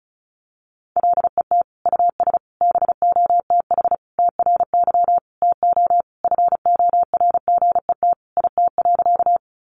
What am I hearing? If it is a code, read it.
Morse code, heard as a keyed tone, about 34 wpm, 700 Hz. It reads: LET US BOTH TRY TO FORGET IT.
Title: Sentences from Top 1200 Words 35wpm